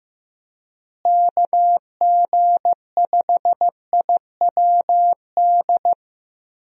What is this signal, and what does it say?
Morse code, heard as a keyed tone, about 15 wpm, 700 Hz. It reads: KG5IWD
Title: USA Call Signs - 6 Characters - Repeated 2x - 15wpm